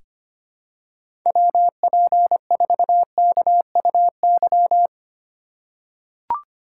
Morse code, heard as a keyed tone, about 25 wpm, 700 Hz. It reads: WP4KUY
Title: USA Call Signs - 6 Characters - 2x Character Spacing - 25wpm